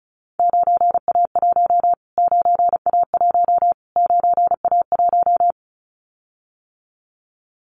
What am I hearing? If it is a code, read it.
Morse code, heard as a keyed tone, about 34 wpm, 700 Hz. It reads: 9A1 9A1 9A1
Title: Callsign Prefixes for Top 25 DXCC - Repeated 3x - 35wpm